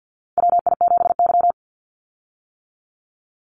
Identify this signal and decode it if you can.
Morse code, heard as a keyed tone, about 50 wpm, 700 Hz. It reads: WI7Y
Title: USA Call Signs 50wpm